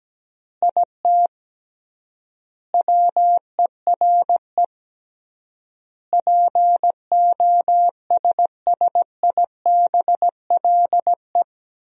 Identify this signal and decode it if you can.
Morse code, heard as a keyed tone, about 17 wpm, 700 Hz. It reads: IT WERE POSSIBLE
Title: 3x Word-Spacing - Sentences from Top 500 Words - 17wpm